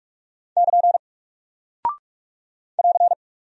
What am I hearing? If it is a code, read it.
Morse code, heard as a keyed tone, about 44 wpm, 700 Hz. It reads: <KN> <AR>
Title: Single Character (Letter:Number:Punctuation:Prosign) Alt 45wpm